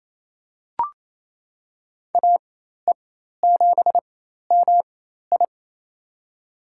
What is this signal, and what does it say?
Morse code, heard as a keyed tone, about 28 wpm, spaced as for 13 wpm, 700 Hz. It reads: AE7MI
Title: USA Call Signs - 5 Characters - 4x Character Spacing - 28wpm